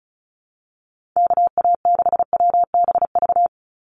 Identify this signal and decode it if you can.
Morse code, heard as a keyed tone, about 36 wpm, 700 Hz. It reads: KA6WBV
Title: USA Call Signs 35wpm